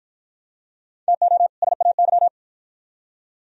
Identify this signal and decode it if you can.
Morse code, heard as a keyed tone, about 49 wpm, 700 Hz. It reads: TO SAY